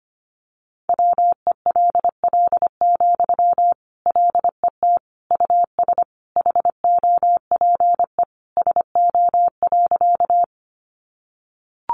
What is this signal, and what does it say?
Morse code, heard as a keyed tone, about 25 wpm, 700 Hz. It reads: WELL, LET US HOPE SO.
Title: Sentences from Top 1000 Words 25wpm